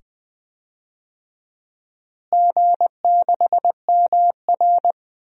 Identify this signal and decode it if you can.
Morse code, heard as a keyed tone, about 20 wpm, 700 Hz. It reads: G6MR
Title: DXCC-100 Call Signs - 4 Characters - Repeated 2x - 20wpm